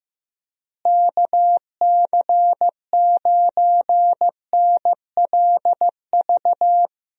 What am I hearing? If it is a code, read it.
Morse code, heard as a keyed tone, about 15 wpm, 700 Hz. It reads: KC9NLV
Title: USA Call Signs - 6 Characters - Repeated 3x - 15wpm